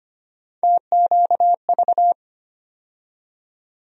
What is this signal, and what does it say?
Morse code, heard as a keyed tone, about 25 wpm, 700 Hz. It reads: TQV
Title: USA Call Sign Postfixes - 3 Letters 25wpm